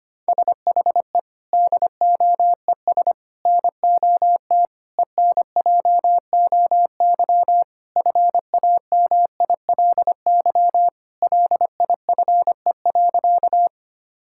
Morse code, heard as a keyed tone, about 25 wpm, 700 Hz. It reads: SHE DOES NOT ENJOY FAMILY LIFE.
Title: Sentences from Top 1000 Words - Review Part 5 - 25wpm